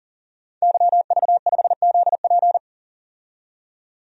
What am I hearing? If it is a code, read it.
Morse code, heard as a keyed tone, about 41 wpm, 700 Hz. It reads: YV5ZP
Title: DXCC-100 Call Signs - 5 Characters - 40wpm